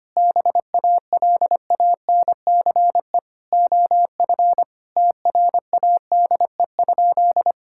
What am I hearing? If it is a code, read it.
Morse code, heard as a keyed tone, about 25 wpm, 700 Hz. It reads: BALANCE OF TRADE?
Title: Sentences from Top 1000 Words - Review Part 6 - 25wpm